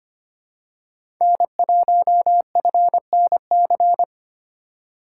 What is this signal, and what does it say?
Morse code, heard as a keyed tone, about 25 wpm, 700 Hz. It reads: N1FNC
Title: USA Call Signs - 5 Characters - 4x Character Spacing - 25wpm